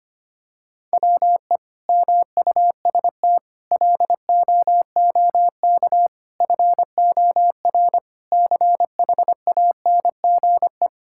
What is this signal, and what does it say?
Morse code, heard as a keyed tone, about 25 wpm, 700 Hz. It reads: WE MUST LOOK FOR CHANGE
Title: Sentences from Top 1600 Words 25wpm